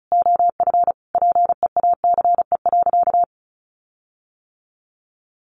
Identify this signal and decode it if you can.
Morse code, heard as a keyed tone, about 34 wpm, 700 Hz. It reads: OF PEACE.
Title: Sentences from Top 1500 Words 35wpm